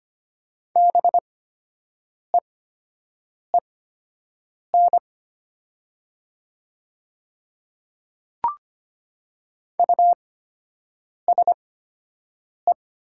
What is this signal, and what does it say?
Morse code, heard as a keyed tone, about 25 wpm, spaced as for 7 wpm, 700 Hz. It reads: BEEN USE
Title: 8x Character Spacing - Top 300 Words - 25wpm